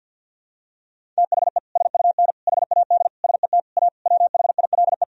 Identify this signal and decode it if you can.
Morse code, heard as a keyed tone, about 46 wpm, 700 Hz. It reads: THE SUN HAD SET A WHILE
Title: Sentences from Top 1000 Words - Review Part 1 - 50wpm